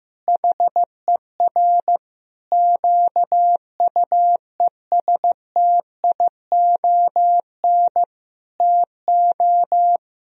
Morse code, heard as a keyed tone, about 15 wpm, 700 Hz. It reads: HER QUESTION TO
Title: Sentences from Top 1000 Words - Review Part 2 - 15wpm